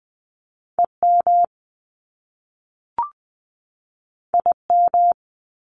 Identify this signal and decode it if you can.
Morse code, heard as a keyed tone, about 20 wpm, 700 Hz. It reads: EM IM
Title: Top Two Letter Combinations in Words 20wpm